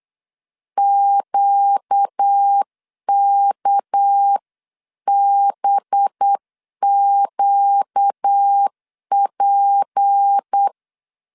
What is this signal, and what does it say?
Morse code, heard as a keyed tone, about 8 wpm, 800 Hz. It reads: QKBQP
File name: CW / DET / 1 6 1500C S C 8wpm 5219